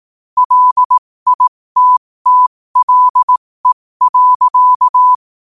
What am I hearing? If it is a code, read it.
Morse code, heard as a keyed tone, about 18 wpm, spaced as for 15 wpm, 1000 Hz. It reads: LITTLE.
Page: 15 wpm morse code quotes